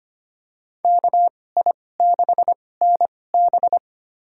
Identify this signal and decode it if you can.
Morse code, heard as a keyed tone, about 25 wpm, spaced as for 18 wpm, 700 Hz. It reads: KI6NB
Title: USA Call Signs - 5 Characters - 2x Character Spacing - 25wpm